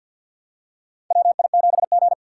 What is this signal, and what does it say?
Morse code, heard as a keyed tone, about 47 wpm, 700 Hz. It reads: WI7G
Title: USA Call Signs - 4 Characters - 3x Character Spacing - 50wpm